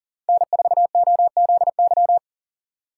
Morse code, heard as a keyed tone, about 40 wpm, 700 Hz. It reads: N4OZY
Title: USA Call Signs - 5 Characters - 2x Character Spacing - 40wpm